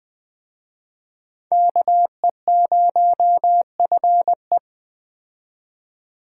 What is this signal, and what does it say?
Morse code, heard as a keyed tone, about 20 wpm, 700 Hz. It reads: KE0FE